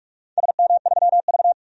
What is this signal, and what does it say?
Morse code, heard as a keyed tone, about 45 wpm, 690 Hz. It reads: SM3V
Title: DXCC-100 Call Signs - 4 Characters - 45wpm